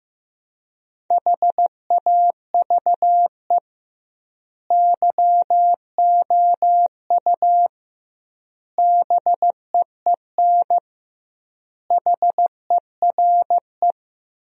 Morse code, heard as a keyed tone, about 15 wpm, 700 Hz. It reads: HAVE YOU BEEN HERE
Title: 2x Word-Spacing - Sentences from Top 200 Words - 15wpm